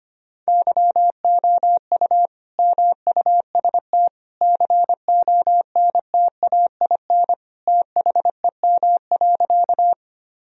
Morse code, heard as a keyed tone, about 25 wpm, 700 Hz. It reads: YOU MUST CONTAIN THEM.
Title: Sentences from Top 1000 Words - Review Part 2 - 25wpm